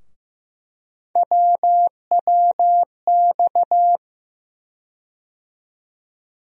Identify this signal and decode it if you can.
Morse code, heard as a keyed tone, about 15 wpm, 700 Hz. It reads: WWX
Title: USA Call Sign Postfixes - 3 Letters 15wpm